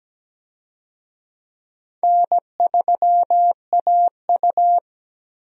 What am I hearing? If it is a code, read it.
Morse code, heard as a keyed tone, about 17 wpm, 700 Hz. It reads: N3AU
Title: USA Call Signs - 4 Characters - Repeated 3x - 17wpm